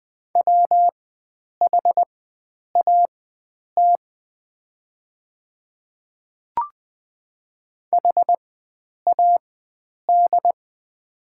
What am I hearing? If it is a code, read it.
Morse code, heard as a keyed tone, about 20 wpm, spaced as for 9 wpm, 700 Hz. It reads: WHAT HAD
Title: 4x Character Spacing - Top 100 Words - 20wpm